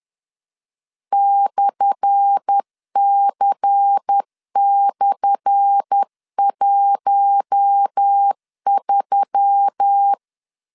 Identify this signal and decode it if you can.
Morse code, heard as a keyed tone, about 10 wpm, 800 Hz. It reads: /C/13
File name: LEZIONI / 10 SIGNS / 10A SLASH 10wpm 2808